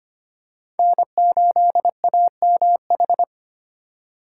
Morse code, heard as a keyed tone, about 25 wpm, 700 Hz. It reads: N8AMH
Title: USA Call Signs - 5 Characters - 2x Character Spacing - 25wpm